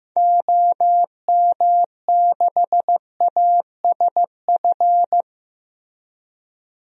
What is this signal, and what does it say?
Morse code, heard as a keyed tone, about 15 wpm, 700 Hz. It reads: OM6ASF
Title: DXCC-100 Call Signs - 6 Characters - Repeated 3x - 15wpm